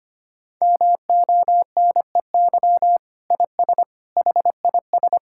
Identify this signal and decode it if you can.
Morse code, heard as a keyed tone, about 25 wpm, 700 Hz. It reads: MONEY IS HIS
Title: Sentences from Top 1600 Words 25wpm